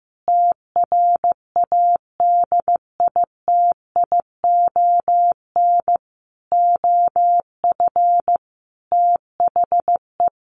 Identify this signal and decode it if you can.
Morse code, heard as a keyed tone, about 15 wpm, 700 Hz. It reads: TRADITION OF THE
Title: Sentences from Top 1500 Words 15wpm